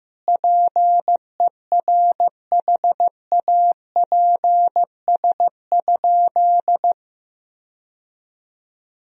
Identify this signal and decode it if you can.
Morse code, heard as a keyed tone, about 15 wpm, 700 Hz. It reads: PERHAPS?
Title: Sentences from Top 1000 Words - Review Part 2 - 15wpm